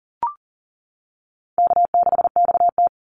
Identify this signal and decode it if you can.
Morse code, heard as a keyed tone, about 40 wpm, 700 Hz. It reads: K6XT
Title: Easy Call Signs - 40wpm